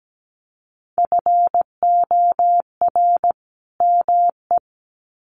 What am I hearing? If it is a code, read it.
Morse code, heard as a keyed tone, about 17 wpm, 700 Hz. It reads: FOR ME
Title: Sets of 2 Words 17wpm